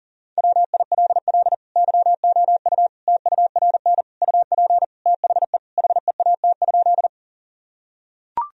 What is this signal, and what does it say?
Morse code, heard as a keyed tone, about 40 wpm, 700 Hz. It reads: WILL YOU TURN UP THE HEAT?
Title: Sentences from Top 1000 Words - Review Part 3 - 40wpm